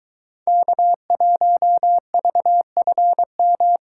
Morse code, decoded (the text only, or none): K1VFM